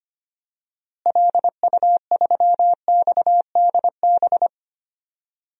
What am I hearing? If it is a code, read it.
Morse code, heard as a keyed tone, about 25 wpm, 700 Hz. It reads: LU3XDB
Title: DXCC-100 Call Signs - 6 Characters - Repeated 3x - 25wpm